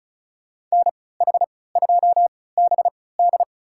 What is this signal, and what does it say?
Morse code, heard as a keyed tone, about 36 wpm, spaced as for 20 wpm, 700 Hz. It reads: NH2BD
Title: USA Call Signs - 5 Characters - 3x Character Spacing - 35wpm